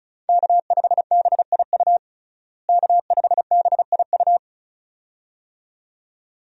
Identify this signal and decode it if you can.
Morse code, heard as a keyed tone, about 36 wpm, 700 Hz. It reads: K5BIU K5BIU
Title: USA Call Signs - 5 Characters - Repeated 2x - 35wpm